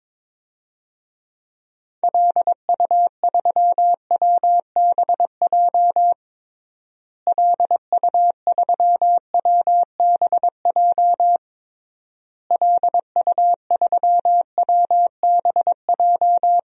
Morse code, heard as a keyed tone, about 22 wpm, 700 Hz. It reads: LU3WBJ LU3WBJ LU3WBJ
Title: DXCC-100 Call Signs - 6 Characters - Repeated 3x - 22wpm